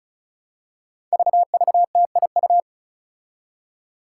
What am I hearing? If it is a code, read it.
Morse code, heard as a keyed tone, about 36 wpm, 700 Hz. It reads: VVTIU
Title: Five Challenging Letter-Number Combinations 35wpm